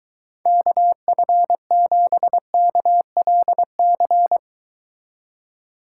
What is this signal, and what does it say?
Morse code, heard as a keyed tone, about 23 wpm, 700 Hz. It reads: KF7KLC